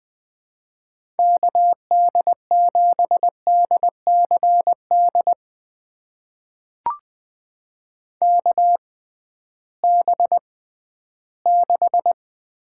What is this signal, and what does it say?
Morse code, heard as a keyed tone, about 20 wpm, spaced as for 13 wpm, 700 Hz. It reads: KD7DCD KB6
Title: USA Call Signs - 6 Characters - 6x Character Spacing - 20wpm